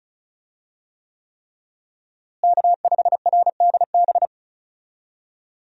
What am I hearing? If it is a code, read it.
Morse code, heard as a keyed tone, about 34 wpm, 700 Hz. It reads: K5RDB
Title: USA Call Signs - 5 Characters - Repeated 3x - 35wpm